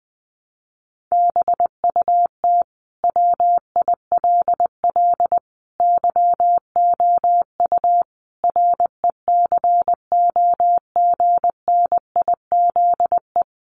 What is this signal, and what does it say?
Morse code, heard as a keyed tone, about 20 wpm, 700 Hz. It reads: BUT WILL YOU RECOGNIZE
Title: Sentences from Top 1400 Words 20wpm